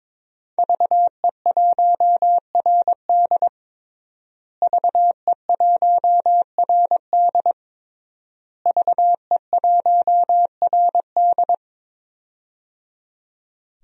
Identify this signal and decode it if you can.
Morse code, heard as a keyed tone, about 22 wpm, 700 Hz. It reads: VE1RD VE1RD VE1RD
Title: DXCC-100 Call Signs - 5 Characters - Repeated 3x - 22wpm